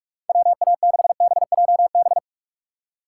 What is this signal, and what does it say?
Morse code, heard as a keyed tone, about 59 wpm, spaced as for 52 wpm, 700 Hz. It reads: WA6BJB
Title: Speed Racing - USA Call Signs 45wpm